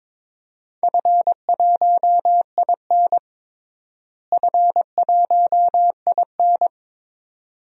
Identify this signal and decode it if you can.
Morse code, heard as a keyed tone, about 22 wpm, 700 Hz. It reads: F1IN F1IN
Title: DXCC-100 Call Signs - 4 Characters - Repeated 2x - 22wpm